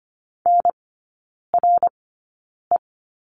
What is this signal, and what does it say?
Morse code, heard as a keyed tone, about 25 wpm, spaced as for 9 wpm, 700 Hz. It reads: NRE
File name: Lesson 15e - Number 5 - Three Characters 25wpm